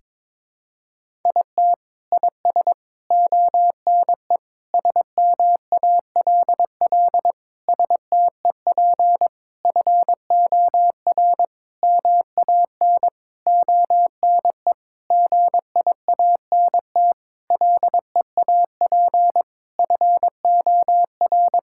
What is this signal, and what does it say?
Morse code, heard as a keyed tone, about 22 wpm, 700 Hz. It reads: IT IS ONE SMALL STEP FOR MAN ONE GIANT LEAP FOR
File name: Sentences with 9-Word Phrases - 22wpm